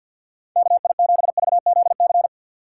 Morse code, heard as a keyed tone, about 47 wpm, 700 Hz. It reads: K I 7 V Z C
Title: USA Call Signs - 6x Character Spacing - 50wpm